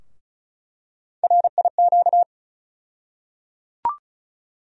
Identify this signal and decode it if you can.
Morse code, heard as a keyed tone, about 34 wpm, 700 Hz. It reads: RIQ
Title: USA Call Sign Postfixes - 3 Letters 35wpm